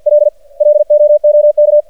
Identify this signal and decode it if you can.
Morse code, heard as a keyed tone, about 50 wpm, 600 Hz. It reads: L LOOK
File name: archived / CRSnet